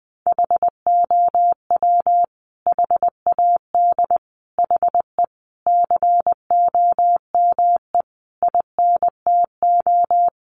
Morse code, heard as a keyed tone, about 20 wpm, 700 Hz. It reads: HOW HAD HE COME INTO
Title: Sentences from Top 1300 Words 20wpm